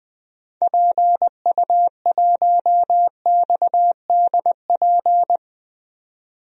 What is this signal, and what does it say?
Morse code, heard as a keyed tone, about 20 wpm, 700 Hz. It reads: PU1XDP
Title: DXCC-100 Call Signs - 6 Characters - 20wpm